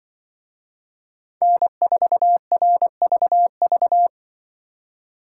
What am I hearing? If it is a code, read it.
Morse code, heard as a keyed tone, about 24 wpm, 700 Hz. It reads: N4RVV
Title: Speed Racing - USA 1x3 Call Signs 20wpm